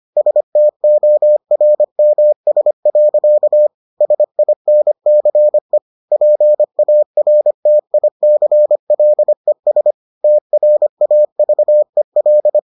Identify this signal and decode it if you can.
Morse code, heard as a keyed tone, about 25 wpm, 600 Hz. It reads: STORMS. SINCE PARTICLES TRAVEL